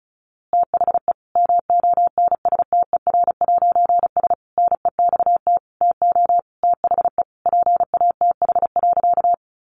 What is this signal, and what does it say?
Morse code, heard as a keyed tone, about 34 wpm, 700 Hz. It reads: THE MONSTER'S NEXT TO THE PATH.
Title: A Sound Of Thunder 35wpm